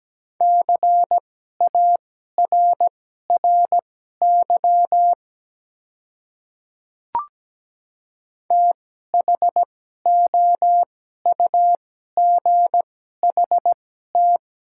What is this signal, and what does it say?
Morse code, heard as a keyed tone, about 17 wpm, spaced as for 12 wpm, 700 Hz. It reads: CARRY THOUGHT
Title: 2x Character Spacing - Top 300 Words - 17wpm